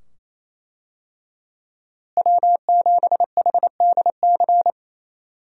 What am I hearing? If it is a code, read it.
Morse code, heard as a keyed tone, about 27 wpm, 700 Hz. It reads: W7HDC
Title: USA Call Signs - 5 Characters - Repeated 2x - 28wpm